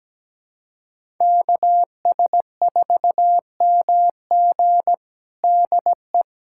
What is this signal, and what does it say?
Morse code, heard as a keyed tone, about 17 wpm, 700 Hz. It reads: KS4MG DE
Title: USA Callsign DE Callsign - 5 Characters - 17wpm